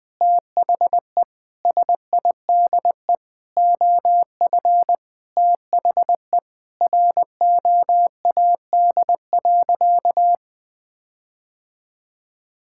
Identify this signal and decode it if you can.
Morse code, heard as a keyed tone, about 20 wpm, 700 Hz. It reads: THE SIDE OF THE ROAD.
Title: Sentences with 5-Word Phrases - 20wpm